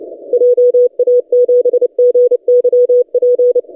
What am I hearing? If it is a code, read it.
Morse code, heard as a keyed tone, about 30 wpm, 490 Hz. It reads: JA7GYP
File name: posts / 3d2rh-20cw- -0330z